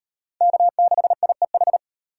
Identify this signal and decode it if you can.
Morse code, heard as a keyed tone, about 35 wpm, 700 Hz. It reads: K6IEH